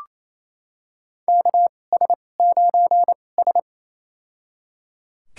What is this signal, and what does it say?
Morse code, heard as a keyed tone, about 28 wpm, spaced as for 20 wpm, 700 Hz. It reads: KS9S
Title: USA Call Signs - 4 Characters - 2x Character Spacing - 28wpm